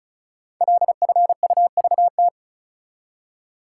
Morse code, heard as a keyed tone, about 36 wpm, 700 Hz. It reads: LFUVT
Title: Code Groups - 5 Letters - 35wpm